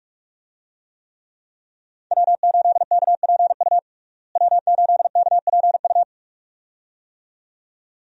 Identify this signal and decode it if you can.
Morse code, heard as a keyed tone, about 46 wpm, 700 Hz. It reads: W8KPU W8KPU
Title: USA Call Signs - 5 Characters - Repeated 2x - 45wpm